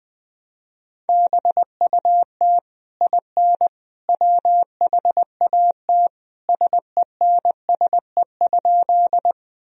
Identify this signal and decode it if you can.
Morse code, heard as a keyed tone, about 20 wpm, 700 Hz. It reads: BUT IN WHAT SENSE?
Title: Sentences from Top 1000 Words - Review Part 4 - 20wpm